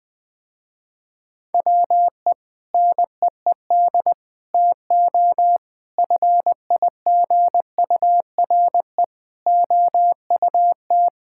Morse code, heard as a keyed tone, about 20 wpm, 700 Hz. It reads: WE NEED TO FIGURE OUT
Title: Sentences with 5-Word Phrases - Spoken Beginning - 20wpm